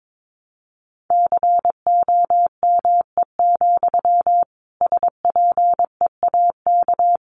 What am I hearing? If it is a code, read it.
Morse code, heard as a keyed tone, about 22 wpm, 700 Hz. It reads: COME, SPEAK
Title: Sentences from Top 1100 Words 22wpm